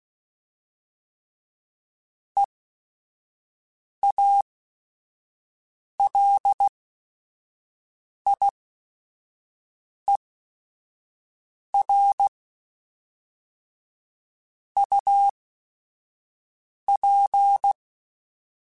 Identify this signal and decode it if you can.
Morse code, heard as a keyed tone, about 16 wpm, spaced as for 5 wpm, 800 Hz. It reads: EALIER UP
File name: Weather Sets / Weather Sets-mp3s / Weather 5 5 @5